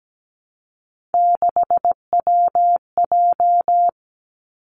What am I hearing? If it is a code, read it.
Morse code, heard as a keyed tone, about 17 wpm, 700 Hz. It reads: 6WJ
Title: Three Challenging Letter-Number Combinations 17wpm